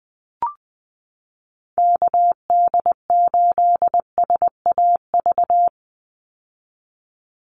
Morse code, heard as a keyed tone, about 20 wpm, 700 Hz. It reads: KD8SAV
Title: Call Signs - 20wpm